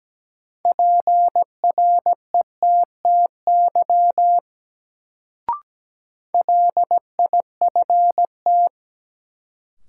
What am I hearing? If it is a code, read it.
Morse code, heard as a keyed tone, about 17 wpm, 700 Hz. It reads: PRETTY LIFT
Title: Sentences from Top 1000 Words - New Word Drill - 17wpm